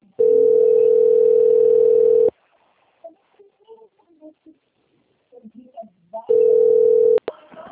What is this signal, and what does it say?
Ringback tone (North America), 440 + 480 Hz.